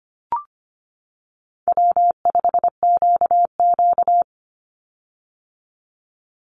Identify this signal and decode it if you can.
Morse code, heard as a keyed tone, about 25 wpm, 700 Hz. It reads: W5QQ
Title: Easy Call Signs - 25wpm